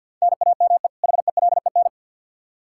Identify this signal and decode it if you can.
Morse code, heard as a keyed tone, about 46 wpm, 700 Hz. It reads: NAME HELEN
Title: Practice QSO - Exchange 1 - Beginning - 50wpm